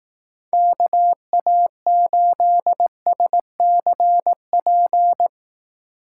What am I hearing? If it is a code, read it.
Morse code, heard as a keyed tone, about 18 wpm, 700 Hz. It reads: KA8SCP
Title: Speed Racing - USA 2x3 Call Signs 15wpm